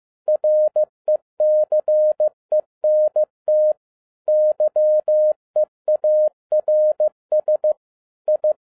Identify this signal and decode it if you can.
Morse code, heard as a keyed tone, about 15 wpm, 610 Hz. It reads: RECENT YEARS I
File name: 2011-07-mp3 / n22 QRP OTAKU 15wpm